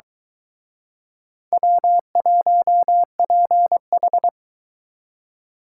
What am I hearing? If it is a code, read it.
Morse code, heard as a keyed tone, about 24 wpm, 700 Hz. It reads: W1PH